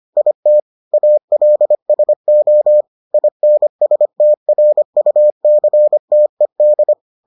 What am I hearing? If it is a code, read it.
Morse code, heard as a keyed tone, about 25 wpm, 600 Hz. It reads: IT ALSO INSTRUCTED